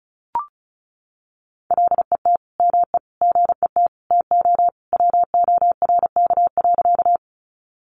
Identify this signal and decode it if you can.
Morse code, heard as a keyed tone, about 34 wpm, 700 Hz. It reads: LET ME GET TO WORK.
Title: Sentences from Top 300 Words 35wpm